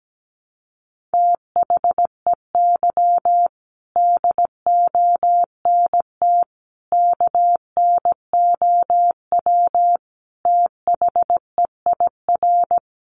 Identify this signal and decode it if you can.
Morse code, heard as a keyed tone, about 17 wpm, 700 Hz. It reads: THEY DONT KNOW THEIR
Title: Sentences from Top 300 Words 17wpm